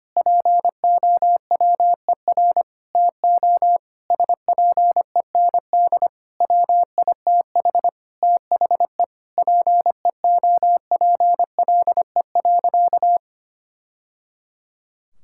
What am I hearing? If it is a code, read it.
Morse code, heard as a keyed tone, about 25 wpm, 700 Hz. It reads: POWER TO SPEND WITH THE PEOPLE.
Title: Sentences from Top 1000 Words - Review Part 5 - 25wpm